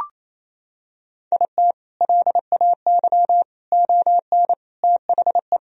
Morse code, heard as a keyed tone, about 28 wpm, 700 Hz. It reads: IT LAY ON THE
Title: Sentences from Top 1000 Words - Review Part 2 - 28wpm